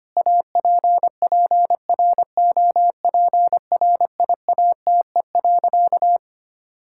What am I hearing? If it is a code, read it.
Morse code, heard as a keyed tone, about 25 wpm, 700 Hz. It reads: APPROPRIATE.
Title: Sentences from Top 1000 Words - Review Part 5 - 25wpm